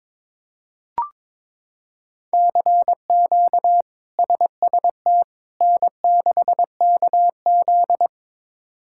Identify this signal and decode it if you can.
Morse code, heard as a keyed tone, about 22 wpm, 700 Hz. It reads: CQ SST N6KZ
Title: K1USN SST QSO Practice - 22wpm